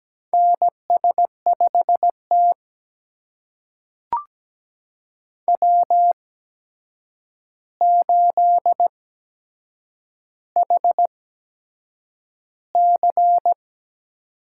Morse code, heard as a keyed tone, about 17 wpm, spaced as for 7 wpm, 700 Hz. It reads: NS5T W8HC
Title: USA Call Signs - 4 Characters - 8x Character Spacing - 17wpm